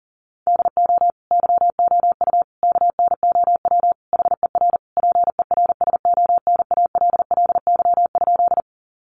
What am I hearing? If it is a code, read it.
Morse code, heard as a keyed tone, about 40 wpm, 700 Hz. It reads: DO YOU KNOW HER PERSONALLY?
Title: Sentences from Top 1400 Words 40wpm